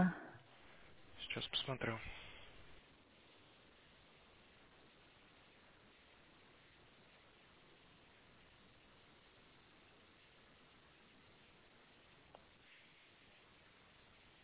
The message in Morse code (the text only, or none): none